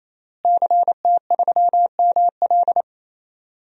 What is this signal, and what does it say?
Morse code, heard as a keyed tone, about 28 wpm, 700 Hz. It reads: CT3ML
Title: DXCC-100 Call Signs - 5 Characters - Repeated 3x - 28wpm